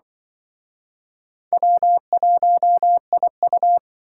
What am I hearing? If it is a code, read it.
Morse code, heard as a keyed tone, about 24 wpm, 700 Hz. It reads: W1IU
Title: Speed Racing - USA Call Signs 20wpm